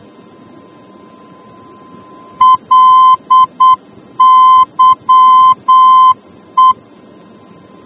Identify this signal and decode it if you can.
Morse code, heard as a keyed tone, about 8 wpm, 1030 Hz. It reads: LYE